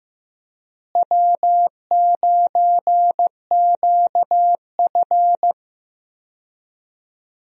Morse code, heard as a keyed tone, about 15 wpm, 700 Hz. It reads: W9QF
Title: USA Call Signs - 4 Characters - Repeated 3x - 15wpm